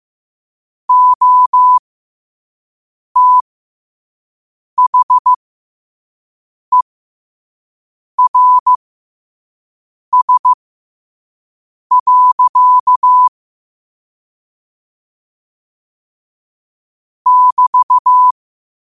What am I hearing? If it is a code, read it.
Morse code, heard as a keyed tone, about 15 wpm, spaced as for 5 wpm, 1000 Hz. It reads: OTHERS. <BT>
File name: qotdfiles / 5wpm